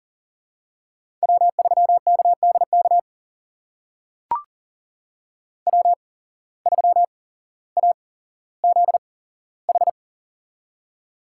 Morse code, heard as a keyed tone, about 40 wpm, spaced as for 17 wpm, 700 Hz. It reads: W3KDK W3AZH
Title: USA Call Signs - 5 Characters - 8x Character Spacing - 40wpm